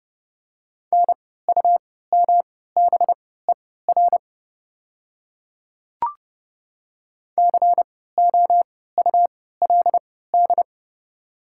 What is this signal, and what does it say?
Morse code, heard as a keyed tone, about 30 wpm, spaced as for 17 wpm, 690 Hz. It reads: NUMBER COULD
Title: 3x Character Spacing - Top 200 Words - 30wpm